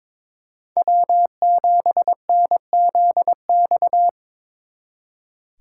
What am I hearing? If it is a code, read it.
Morse code, heard as a keyed tone, about 26 wpm, 700 Hz. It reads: W7NZX
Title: Speed Racing - USA Call Signs 22wpm